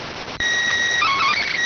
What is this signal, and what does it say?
SSTV (slow-scan television), PD 290.